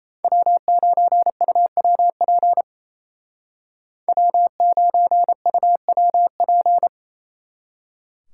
Morse code, heard as a keyed tone, about 34 wpm, 700 Hz. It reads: W9UWP W9UWP
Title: Speed Racing - USA 1x3 Call Signs 28wpm